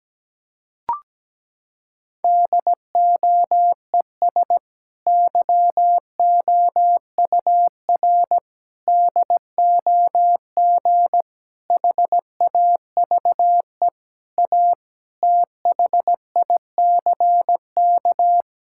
Morse code, heard as a keyed tone, about 17 wpm, 700 Hz. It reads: DOES YOUR DOG HAVE A THICK
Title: Sentences from Top 1600 Words 17wpm